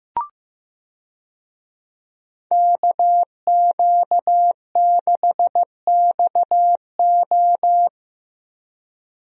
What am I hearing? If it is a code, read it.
Morse code, heard as a keyed tone, about 15 wpm, 700 Hz. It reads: KQ6XO
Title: USA Call Signs - 5 Characters - Repeated 2x - 15wpm